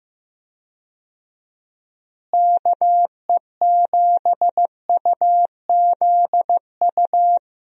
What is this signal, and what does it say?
Morse code, heard as a keyed tone, about 15 wpm, 700 Hz. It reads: KE7UZU
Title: USA Call Signs - 6 Characters - Repeated 3x - 15wpm